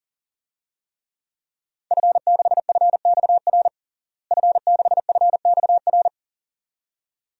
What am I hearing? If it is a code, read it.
Morse code, heard as a keyed tone, about 40 wpm, 700 Hz. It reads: F6FXR F6FXR
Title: DXCC-100 Call Signs - 5 Characters - Repeated 2x - 40wpm